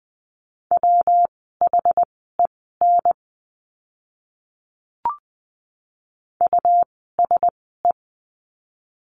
Morse code, heard as a keyed tone, about 20 wpm, spaced as for 14 wpm, 700 Hz. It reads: WHEN USE
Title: 115 Lesson 19f - Letter w - Words 20wpm